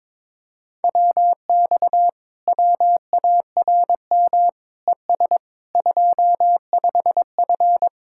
Morse code, heard as a keyed tone, about 22 wpm, 700 Hz. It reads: WX WARM ES 25F
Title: Speed-Racing Common QSO Phrases - 22wpm